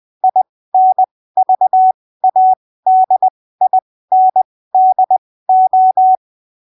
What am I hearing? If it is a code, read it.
Morse code, heard as a keyed tone, about 20 wpm, spaced as for 15 wpm, 750 Hz. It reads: INVADINDO
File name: arquivos / texto / texto 25 20ppm